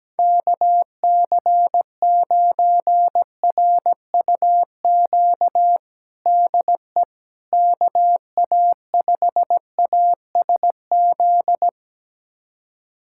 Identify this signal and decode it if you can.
Morse code, heard as a keyed tone, about 17 wpm, 700 Hz. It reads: KC9RUQ DE KA5ASZ
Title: USA Callsign DE Callsign - 6 Characters - 17wpm